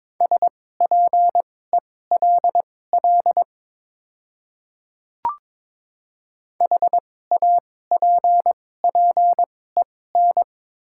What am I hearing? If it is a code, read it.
Morse code, heard as a keyed tone, about 22 wpm, spaced as for 16 wpm, 700 Hz. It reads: SPELL HAPPEN
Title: 2x Character Spacing - Top 500 Words - 22wpm